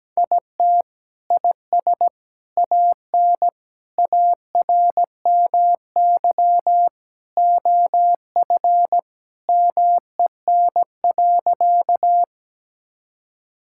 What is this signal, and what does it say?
Morse code, heard as a keyed tone, about 17 wpm, 700 Hz. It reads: IT IS AN ARMY OF MEN.
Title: Sentences from Top 1000 Words - Review Part 5 - 17wpm